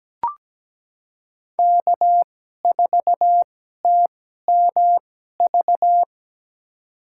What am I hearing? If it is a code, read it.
Morse code, heard as a keyed tone, about 17 wpm, spaced as for 12 wpm, 700 Hz. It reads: K4TMV
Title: USA Call Signs - 5 Characters - 2x Character Spacing - 17wpm